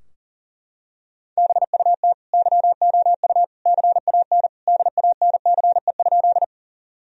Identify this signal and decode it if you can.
Morse code, heard as a keyed tone, about 40 wpm, 700 Hz. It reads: BUT YOU CAN DANCE?
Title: Sentences from Top 1000 Words - Review Part 4 - 40wpm